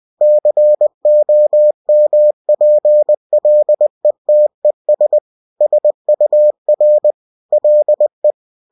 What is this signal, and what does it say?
Morse code, heard as a keyed tone, about 20 wpm, 600 Hz. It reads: COMPLETES SUR LE